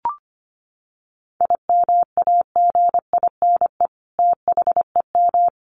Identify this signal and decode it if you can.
Morse code, heard as a keyed tone, about 25 wpm, 700 Hz. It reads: IMAGINE THEM
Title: Sentences from Top 1400 Words 25wpm